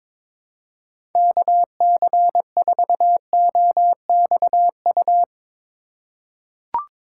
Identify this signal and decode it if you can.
Morse code, heard as a keyed tone, about 22 wpm, 700 Hz. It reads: KC4OXU
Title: USA Call Signs - 6 Characters - 3x Character Spacing - 22wpm